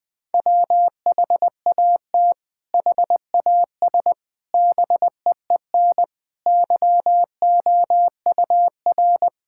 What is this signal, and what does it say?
Morse code, heard as a keyed tone, about 21 wpm, 700 Hz. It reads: WHAT HAS BEEN YOUR
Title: Sentences from Top 1000 Words - Review Part 6 - 20wpm